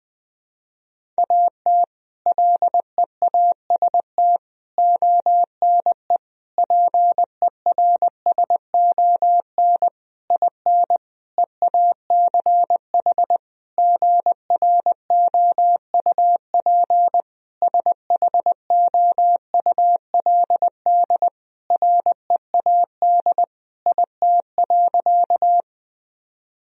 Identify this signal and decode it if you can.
Morse code, heard as a keyed tone, about 20 wpm, 700 Hz. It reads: AT LEAST ONE PERSON IN EACH GROUP SHOULD READ IT.
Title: Sentences from Top 1000 Words - Review Part 1 - 20wpm